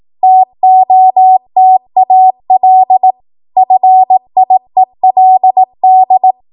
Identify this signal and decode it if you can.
Morse code, heard as a keyed tone, about 18 wpm, 750 Hz. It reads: TOTAL FIELD
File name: Morse / 18WPM